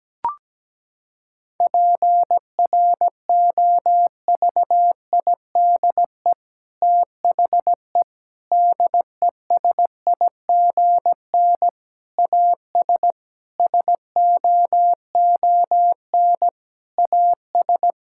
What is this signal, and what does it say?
Morse code, heard as a keyed tone, about 17 wpm, 700 Hz. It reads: PROVIDE THE DESIGN AS SOON AS
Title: Sentences from Top 1000 Words - Review Part 5 - 17wpm